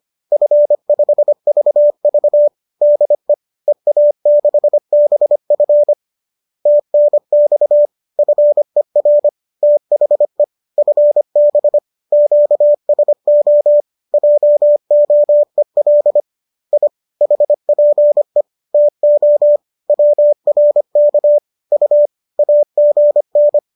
Morse code, heard as a keyed tone, about 25 wpm, 590 Hz. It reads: F5VV DE EA6BF TNX FER THE FB QSO JOEL I HPE TO WRK U AGN